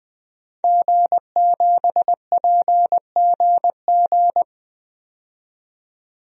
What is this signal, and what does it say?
Morse code, heard as a keyed tone, about 20 wpm, 700 Hz. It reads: G7PGG
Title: DXCC-100 Call Signs - 5 Characters - 20wpm